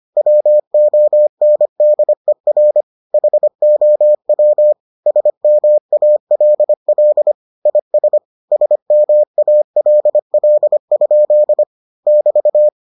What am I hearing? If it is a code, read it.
Morse code, heard as a keyed tone, about 25 wpm, 600 Hz. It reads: WONDER HOW SMALL IS SMALL? <BT>